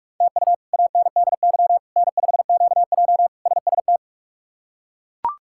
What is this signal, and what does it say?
Morse code, heard as a keyed tone, about 42 wpm, 700 Hz. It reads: TU ANDY N5QJ SST
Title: K1USN SST QSO Practice - 45wpm